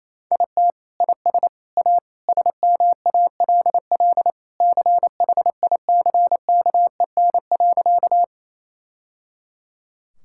Morse code, heard as a keyed tone, about 28 wpm, 700 Hz. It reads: IT IS A SMALL CHICKEN.
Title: Sentences from Top 1000 Words - Review Part 5 - 28wpm